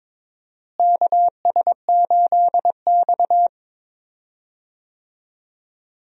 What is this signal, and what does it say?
Morse code, heard as a keyed tone, about 22 wpm, 700 Hz. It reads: KS8X
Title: USA Call Signs - 4 Characters - Repeated 3x - 22wpm